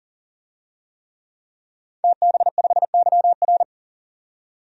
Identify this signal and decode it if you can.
Morse code, heard as a keyed tone, about 40 wpm, 700 Hz. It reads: TB5YR
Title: DXCC-100 Call Signs - 5 Characters - 40wpm